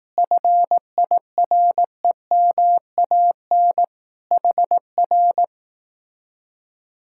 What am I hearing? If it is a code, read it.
Morse code, heard as a keyed tone, about 18 wpm, 700 Hz. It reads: FIREMAN HR
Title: Speed-Racing Common QSO Phrases Encore - 15wpm